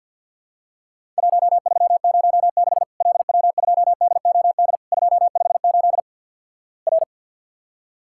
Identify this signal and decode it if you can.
Morse code, heard as a keyed tone, about 50 wpm, 700 Hz. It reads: 1306 LW2DOD 258 R
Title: Worked All Europe DX Contest - QTC Exchange with Repeats - 50wpm